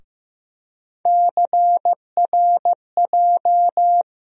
Morse code, heard as a keyed tone, about 15 wpm, 700 Hz. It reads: CRJ
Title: USA Call Sign Postfixes - 3 Letters 15wpm